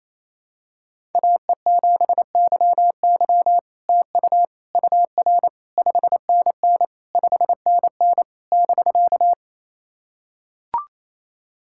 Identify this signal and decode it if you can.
Morse code, heard as a keyed tone, about 27 wpm, 700 Hz. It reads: AE7YY TU UR 5NN 5NN <BK>
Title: SOTA QSO Practice 28wpm